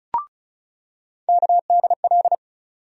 Morse code, heard as a keyed tone, about 36 wpm, 700 Hz. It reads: KDL
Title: USA Call Sign Postfixes - 3 Letters 35wpm